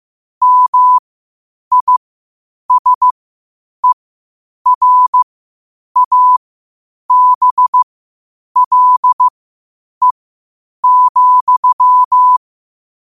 Morse code, heard as a keyed tone, about 15 wpm, spaced as for 8 wpm, 1000 Hz. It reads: MISERABLE,
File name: qotdfiles / 8wpm